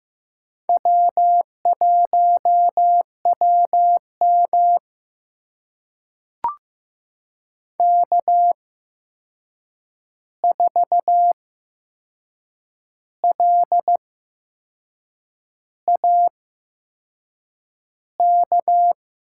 Morse code, heard as a keyed tone, about 15 wpm, spaced as for 6 wpm, 700 Hz. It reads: W1WM K4LAK
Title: USA Call Signs - 8x Character Spacing - 15wpm